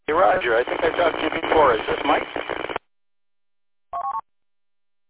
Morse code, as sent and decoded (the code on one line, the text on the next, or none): none